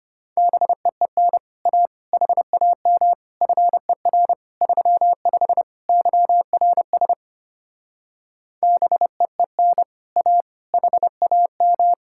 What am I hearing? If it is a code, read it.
Morse code, heard as a keyed tone, about 33 wpm, 700 Hz. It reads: BEEN A HAM FER 35 YRS BEEN A HAM